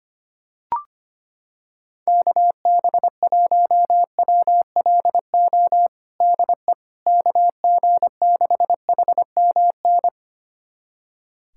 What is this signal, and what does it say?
Morse code, heard as a keyed tone, about 25 wpm, 700 Hz. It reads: KB1WLO DE KG6HMN
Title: USA Callsign DE Callsign 25wpm